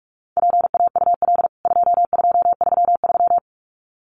Morse code, heard as a keyed tone, about 44 wpm, 700 Hz. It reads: PAUL 2233
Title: CWT Exchange 45wpm